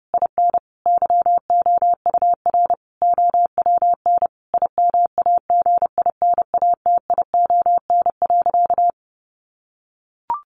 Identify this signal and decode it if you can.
Morse code, heard as a keyed tone, about 30 wpm, 700 Hz. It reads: IN YOUR OWN IMAGINATION.
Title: Sentences from Top 1300 Words 30wpm